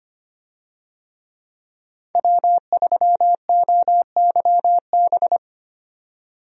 Morse code, heard as a keyed tone, about 25 wpm, 700 Hz. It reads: W3OYB
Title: USA Call Signs - 5 Characters - 25wpm